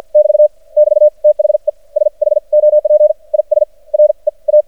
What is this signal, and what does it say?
Morse code, heard as a keyed tone, about 58 wpm, spaced as for 41 wpm, 600 Hz. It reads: <BT> <BT> THE SHOP IS REA
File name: archived / CRSnet